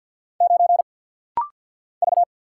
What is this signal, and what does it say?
Morse code, heard as a keyed tone, about 48 wpm, 690 Hz. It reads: 9 V
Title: Single Letter-Number - Rapid-Fire 50wpm